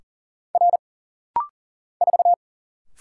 Morse code, heard as a keyed tone, about 40 wpm, 700 Hz. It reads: R 4
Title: Single Letter-Number Alt - Rapid-Fire 40wpm